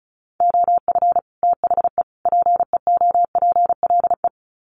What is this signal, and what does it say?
Morse code, heard as a keyed tone, about 34 wpm, 700 Hz. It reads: OF THE PEOPLE
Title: Sets of 3 Words 35wpm